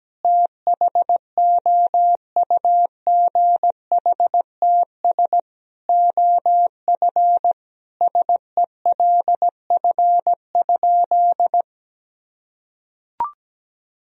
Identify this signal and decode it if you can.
Morse code, heard as a keyed tone, about 17 wpm, 700 Hz. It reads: THOUGHTS OF SELF?
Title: Sentences from Top 1000 Words - Review Part 1 - 17wpm